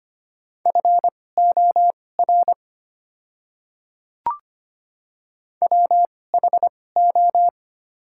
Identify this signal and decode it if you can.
Morse code, heard as a keyed tone, about 25 wpm, spaced as for 18 wpm, 700 Hz. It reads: FOR WHO
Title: 2x Character Spacing - Top 100 Words - 25wpm